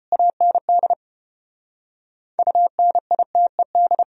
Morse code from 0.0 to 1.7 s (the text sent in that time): AND